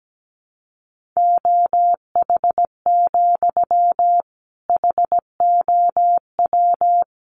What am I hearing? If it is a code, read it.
Morse code, heard as a keyed tone, about 17 wpm, 700 Hz. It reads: OH, HOW
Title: Sentences from Top 500 Words 17wpm